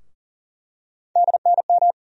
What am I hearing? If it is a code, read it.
Morse code, heard as a keyed tone, about 38 wpm, 700 Hz. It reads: DNM
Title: USA Call Sign Postfixes - 3 Letters 40wpm